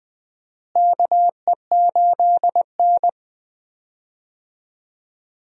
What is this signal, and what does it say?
Morse code, heard as a keyed tone, about 21 wpm, 700 Hz. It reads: KE8N
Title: USA Call Signs - 4 Characters - Repeated 3x - 20wpm